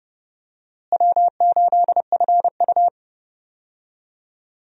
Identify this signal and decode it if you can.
Morse code, heard as a keyed tone, about 30 wpm, 700 Hz. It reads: W8FU
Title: USA Call Signs - 4 Characters - Repeated 2x - 30wpm